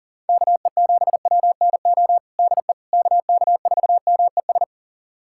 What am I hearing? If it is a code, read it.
Morse code, heard as a keyed tone, about 37 wpm, 700 Hz. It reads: KE7WNO DE KK4MES
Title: USA Callsign DE Callsign - 6 Characters - 40wpm